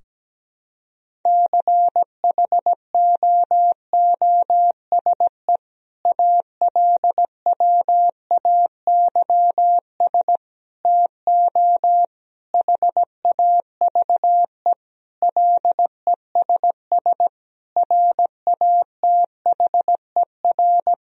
Morse code, heard as a keyed tone, about 17 wpm, 700 Hz. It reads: CHOOSE ALWAYS TO HAVE LESS RATHER
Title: Sentences from Top 1000 Words - Review Part 4 - 17wpm